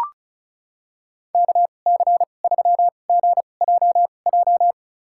Morse code, heard as a keyed tone, about 36 wpm, spaced as for 25 wpm, 700 Hz. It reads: KC3GJJ
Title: USA Call Signs - 6 Characters - 2x Character Spacing - 35wpm